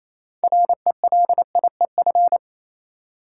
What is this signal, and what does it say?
Morse code, heard as a keyed tone, about 28 wpm, 700 Hz. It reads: RELIEF
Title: Six Letter Words - 28wpm